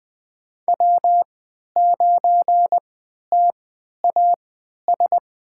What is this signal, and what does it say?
Morse code, heard as a keyed tone, about 20 wpm, spaced as for 11 wpm, 700 Hz. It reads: W9TAS
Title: USA Call Signs - 5 Characters - 3x Character Spacing - 20wpm